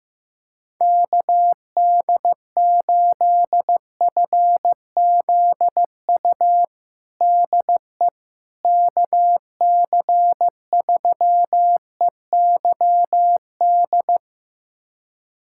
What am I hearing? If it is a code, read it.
Morse code, heard as a keyed tone, about 15 wpm, 700 Hz. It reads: KD8FZU DE KC3EYD
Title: USA Callsign DE Callsign 15wpm